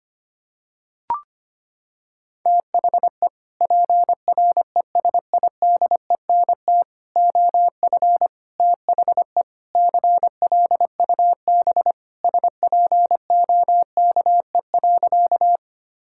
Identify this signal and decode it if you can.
Morse code, heard as a keyed tone, about 25 wpm, 700 Hz. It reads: THE PRESIDENT OF THE CLUB SPOKE.
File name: practice / 25wpm / Sentences with 4-Word Phrases - 25wpm